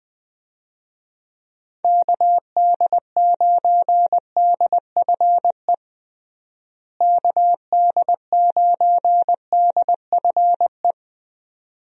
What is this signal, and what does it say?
Morse code, heard as a keyed tone, about 20 wpm, 700 Hz. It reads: KD9DFE KD9DFE
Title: USA Call Signs - 6 Characters - Repeated 2x - 20wpm